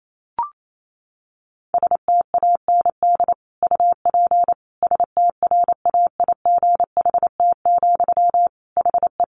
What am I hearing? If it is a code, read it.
Morse code, heard as a keyed tone, about 28 wpm, 700 Hz. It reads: STAND UP STRAIGHT, HE
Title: Sentences from Top 1000 Words 28wpm